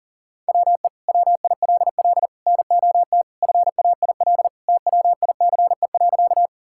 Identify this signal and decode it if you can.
Morse code, heard as a keyed tone, about 40 wpm, 700 Hz. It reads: WE WILL NOT FAIL TWICE.
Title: Sentences from Top 1600 Words 40wpm